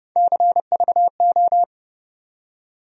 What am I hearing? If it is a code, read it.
Morse code, heard as a keyed tone, about 30 wpm, 700 Hz. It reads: CVO
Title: USA Call Sign Postfixes - 3 Letters 30wpm